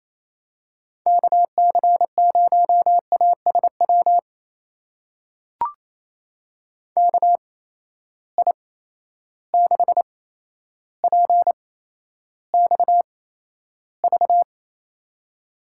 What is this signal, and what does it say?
Morse code, heard as a keyed tone, about 28 wpm, spaced as for 12 wpm, 700 Hz. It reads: KC0ASW KI6PXV
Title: USA Call Signs - 6 Characters - 8x Character Spacing - 28wpm